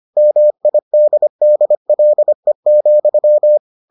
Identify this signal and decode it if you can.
Morse code, heard as a keyed tone, about 25 wpm, 600 Hz. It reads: MIDDLE,